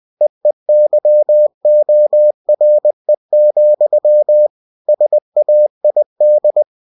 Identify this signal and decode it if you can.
Morse code, heard as a keyed tone, about 20 wpm, 600 Hz. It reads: EEYORE, SAID
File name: cw-books / winnie-the-pooh / winnie-20wpm-page-0088